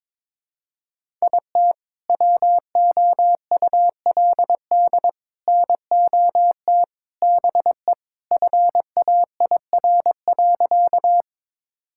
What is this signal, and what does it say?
Morse code, heard as a keyed tone, about 21 wpm, 700 Hz. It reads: IT WOULD NOT BE FAIR.
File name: practice / 22wpm / Sentences with 3-Word Phrases - 22wpm